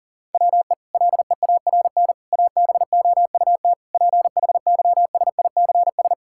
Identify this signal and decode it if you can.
Morse code, heard as a keyed tone, about 40 wpm, 700 Hz. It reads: WE LEARN ABOUT PHYSICS
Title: Sentences from Top 1000 Words - Review Part 5 - 40wpm